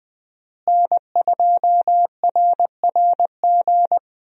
Morse code, heard as a keyed tone, about 20 wpm, 700 Hz. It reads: N2RRG
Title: USA Call Signs - 5 Characters - Repeated 3x - 20wpm